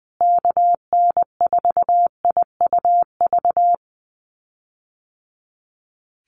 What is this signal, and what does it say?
Morse code, heard as a keyed tone, about 20 wpm, 700 Hz. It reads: KN4IUV
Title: USA Call Signs Repeated 3x 20wpm